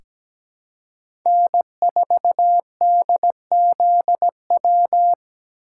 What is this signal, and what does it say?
Morse code, heard as a keyed tone, about 17 wpm, 700 Hz. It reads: N4DZW
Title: USA Call Signs - 5 Characters - 4x Character Spacing - 17wpm